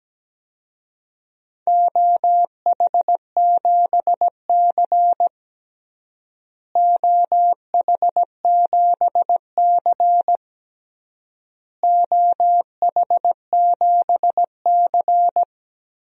Morse code, heard as a keyed tone, about 17 wpm, 700 Hz. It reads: OH7C OH7C OH7C
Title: DXCC-100 Call Signs - 4 Characters - Repeated 3x - 17wpm